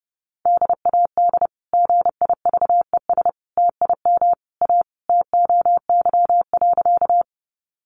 Morse code, heard as a keyed tone, about 28 wpm, 700 Hz. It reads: DAD GIVES TIM A TOY.
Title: Easy Phrases 30wpm